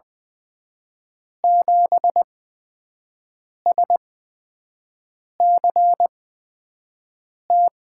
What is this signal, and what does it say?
Morse code, heard as a keyed tone, about 20 wpm, spaced as for 5 wpm, 700 Hz. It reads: 7SCT
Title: USA Call Signs - 8x Character Spacing - 20wpm